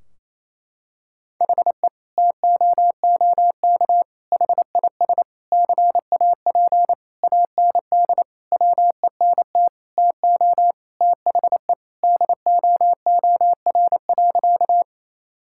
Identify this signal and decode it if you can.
Morse code, heard as a keyed tone, about 28 wpm, 700 Hz. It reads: HE TOOK HIS CAP AND WENT TO THE DOOR.
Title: Sentences from Top 1600 Words 28wpm